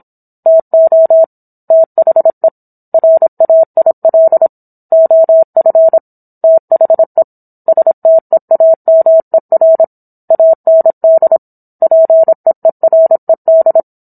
Morse code, heard as a keyed tone, about 27 wpm, 650 Hz. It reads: TO THE RAIL OF THE STEAMER AND PEERED
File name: Book One / 17-Chapter 17 - 26 WPM